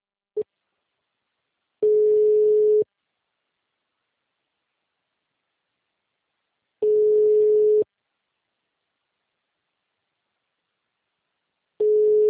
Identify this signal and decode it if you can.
Ringback tone (Europe, 425 Hz).